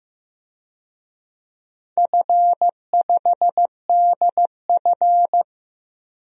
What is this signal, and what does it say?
Morse code, heard as a keyed tone, about 15 wpm, 700 Hz. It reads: F5DF
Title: DXCC-100 Call Signs - 4 Characters - Repeated 3x - 15wpm